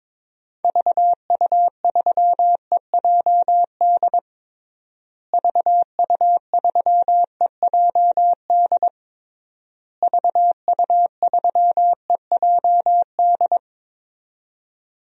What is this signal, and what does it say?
Morse code, heard as a keyed tone, about 22 wpm, 700 Hz. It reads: VU3EJD VU3EJD VU3EJD
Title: DXCC-100 Call Signs - 6 Characters - Repeated 3x - 22wpm